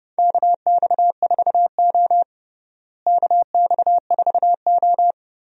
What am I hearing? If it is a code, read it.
Morse code, heard as a keyed tone, about 30 wpm, 700 Hz. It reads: KX4O KX4O
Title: USA Call Signs - 4 Characters - Repeated 2x - 30wpm